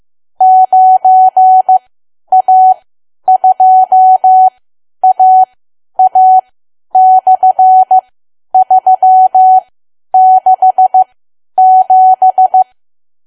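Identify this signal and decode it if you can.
Morse code, heard as a keyed tone, about 15 wpm, spaced as for 10 wpm, 750 Hz. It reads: 9A2AA/367
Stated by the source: Morse learning practice at 10 wpm.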